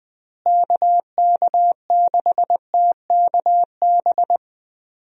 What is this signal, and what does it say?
Morse code, heard as a keyed tone, about 20 wpm, 700 Hz. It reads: KK6TKB
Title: USA Call Signs - 6 Characters - Repeated 3x - 20wpm